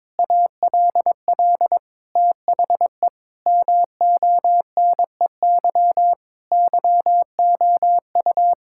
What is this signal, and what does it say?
Morse code, heard as a keyed tone, about 22 wpm, 700 Hz. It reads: ALL THE MONEY YOU
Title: Sentences from Top 1000 Words - Review Part 6 - 22wpm